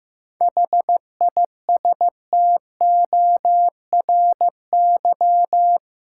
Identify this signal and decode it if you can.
Morse code, heard as a keyed tone, about 15 wpm, 700 Hz. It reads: HISTORY
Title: Sentences from Top 600 Words - New Word Drill - 15wpm